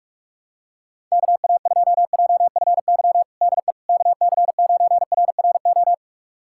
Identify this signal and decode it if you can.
Morse code, heard as a keyed tone, about 41 wpm, 700 Hz. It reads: KA2JFY DE KC9RRO
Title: USA Callsign DE Callsign - 6 Characters - 45wpm